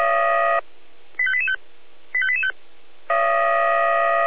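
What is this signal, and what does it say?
Five-tone selective call (ZVEI), heard as quick runs of five tones, tones of about 70 ms. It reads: 85885 85885
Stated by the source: ZVEI five-tone call received on an SDR, Feuerwehr Orlamünde fire brigade.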